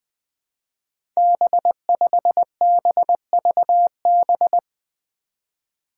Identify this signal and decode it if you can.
Morse code, heard as a keyed tone, about 21 wpm, 700 Hz. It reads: B5BVB
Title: Five Challenging Letter-Number Combinations 20wpm